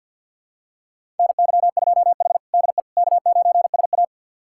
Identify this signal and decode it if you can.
Morse code, heard as a keyed tone, about 46 wpm, 700 Hz. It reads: NY2H DE K9SA
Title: USA Callsign DE Callsign - 4 Characters - 50wpm